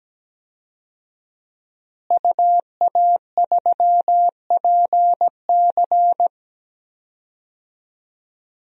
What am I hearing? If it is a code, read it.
Morse code, heard as a keyed tone, about 17 wpm, 700 Hz. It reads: UA3PC
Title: DXCC-100 Call Signs - 5 Characters - Repeated 3x - 17wpm